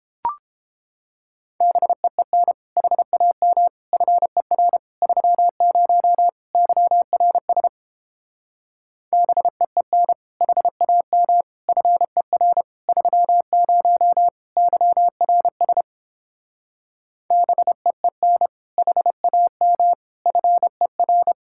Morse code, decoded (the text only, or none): BEEN HAM FER 30 YRS BEEN HAM FER 30 YRS BEEN HAM FER